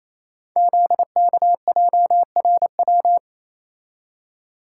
Morse code, heard as a keyed tone, about 29 wpm, 700 Hz. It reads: ZKJRW
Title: Code Groups - 5 Letter:Numbers - 28wpm